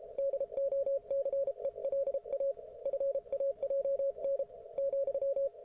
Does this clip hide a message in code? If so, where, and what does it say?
Yes: Morse code from 0.2 to 5.5 s, heard as a keyed tone, about 33 wpm, 560 Hz, with radio noise. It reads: DOCELA FAJN ,